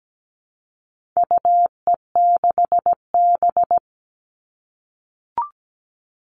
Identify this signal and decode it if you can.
Morse code, heard as a keyed tone, about 17 wpm, 700 Hz. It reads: UE6B
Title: All Cut Numbers - 4-Digit 17wpm